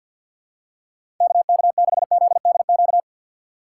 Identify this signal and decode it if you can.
Morse code, heard as a keyed tone, about 49 wpm, 700 Hz. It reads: KK6ZDQ
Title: USA Call Signs - 6 Characters - 4x Character Spacing - 50wpm